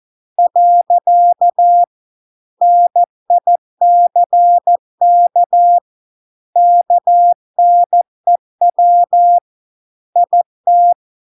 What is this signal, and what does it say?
Morse code, heard as a keyed tone, about 14 wpm, 700 Hz. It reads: . NICK KNEW IT